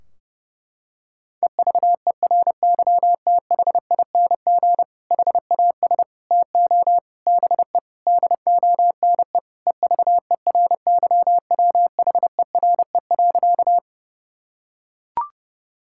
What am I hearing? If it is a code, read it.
Morse code, heard as a keyed tone, about 32 wpm, 700 Hz. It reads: EVERYTHING HAS TO BE DONE EVERYWHERE.
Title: Sentences from Top 1600 Words 30wpm